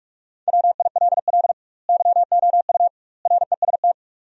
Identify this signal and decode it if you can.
Morse code, heard as a keyed tone, about 46 wpm, 700 Hz. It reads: WILL YOU REST
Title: 2x Word-Spacing - Sentences from Top 500 Words - 45wpm